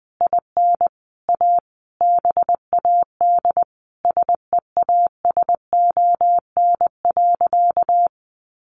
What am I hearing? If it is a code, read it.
Morse code, heard as a keyed tone, about 20 wpm, 700 Hz. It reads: IN A BAD SEASON.
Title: Sentences from Top 1400 Words 20wpm